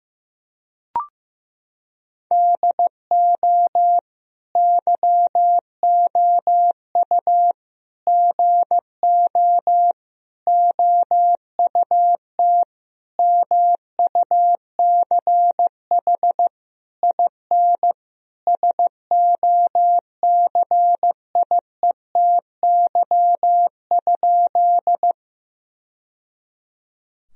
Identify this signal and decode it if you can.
Morse code, heard as a keyed tone, about 15 wpm, 700 Hz. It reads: DO YOU GO OUT MUCH IN SOCIETY?
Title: Sentences from Top 1000 Words - Review Part 4 - 15wpm